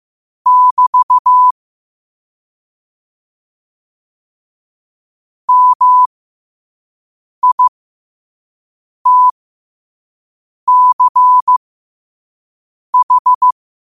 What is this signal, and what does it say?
Morse code, heard as a keyed tone, about 15 wpm, spaced as for 5 wpm, 1000 Hz. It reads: <BT> MITCH
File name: qotdfiles / 5wpm